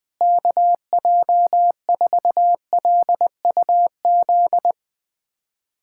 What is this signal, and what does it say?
Morse code, heard as a keyed tone, about 20 wpm, 700 Hz. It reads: KJ4LUZ
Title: USA Callsign DE Callsign - 6 Characters - 20wpm